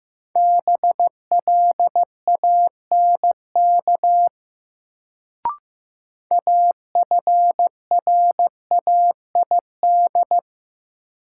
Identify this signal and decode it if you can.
Morse code, heard as a keyed tone, about 15 wpm, 700 Hz. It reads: BLANK AFRAID
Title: Sentences from Top 1500 Words - New Word Drill - 15wpm